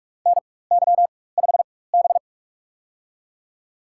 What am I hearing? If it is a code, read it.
Morse code, heard as a keyed tone, about 44 wpm, spaced as for 21 wpm, 700 Hz. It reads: NY5B
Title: USA Call Signs - 4x Character Spacing - 45wpm